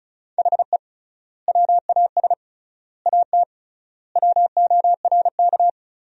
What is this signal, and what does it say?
Morse code, heard as a keyed tone, about 36 wpm, 700 Hz. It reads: HE WAS AT WORK
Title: 3x Word-Spacing - Sentences from Top 200 Words - 35wpm